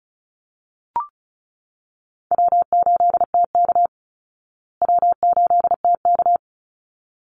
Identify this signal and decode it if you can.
Morse code, heard as a keyed tone, about 34 wpm, 700 Hz. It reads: W8TK W8TK
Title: Easy Call Signs Repeated 2x - 35wpm